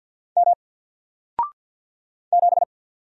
Morse code, heard as a keyed tone, about 50 wpm, 700 Hz. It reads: M 7
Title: Single Letter-Number Alt 50wpm